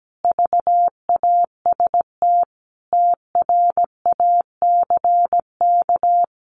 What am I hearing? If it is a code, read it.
Morse code, heard as a keyed tone, about 17 wpm, 700 Hz. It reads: VAST TRACK
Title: Sentences from Top 1300 Words 17wpm